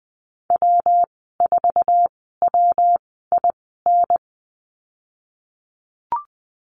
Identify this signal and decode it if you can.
Morse code, heard as a keyed tone, about 20 wpm, spaced as for 15 wpm, 700 Hz. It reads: W4WIN
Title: 173 Lesson 28g - Number 9 - Call Signs 20wpm